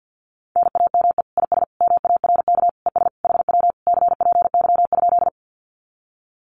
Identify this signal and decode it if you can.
Morse code, heard as a keyed tone, about 48 wpm, 700 Hz. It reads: NAME IS MARK ES HW CPY?
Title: Common QSO Phrases Encore 50wpm